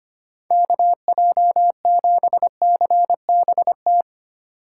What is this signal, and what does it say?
Morse code, heard as a keyed tone, about 25 wpm, 700 Hz. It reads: KJ7CBT
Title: USA Call Signs - 4x Character Spacing - 25wpm